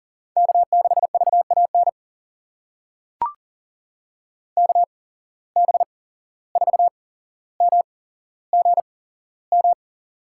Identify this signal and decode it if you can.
Morse code, heard as a keyed tone, about 40 wpm, spaced as for 16 wpm, 700 Hz. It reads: K6VAN KB4MGM
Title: USA Call Signs - 8x Character Spacing - 40wpm